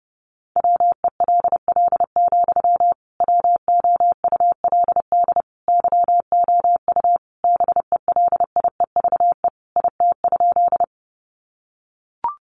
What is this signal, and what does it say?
Morse code, heard as a keyed tone, about 30 wpm, 700 Hz. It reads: WELL, WOULD YOU BELIEVE IT?
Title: Sentences from Top 1000 Words 30wpm